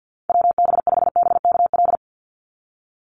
Morse code, heard as a keyed tone, about 50 wpm, 700 Hz. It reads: WB5BKL
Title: NA SOTA Chaser Callsigns 50wpm